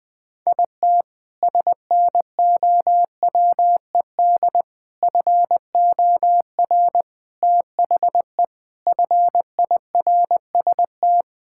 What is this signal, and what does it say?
Morse code, heard as a keyed tone, about 20 wpm, 700 Hz. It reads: IT SNOWED FOR THE FIRST
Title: Sentences with 6-Word Phrases - 20wpm